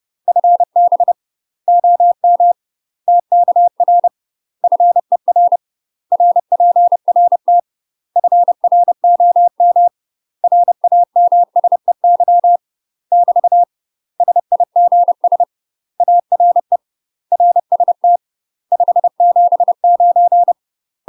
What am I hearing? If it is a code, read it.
Morse code, heard as a keyed tone, about 30 wpm, 700 Hz. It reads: FB OM TKR FER RPRT FROM RAMSEY <BT> SIGS ARE RST 579
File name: UK MASTER QSOs TRAINING / 30WPM / UK MASTER QSOs 0039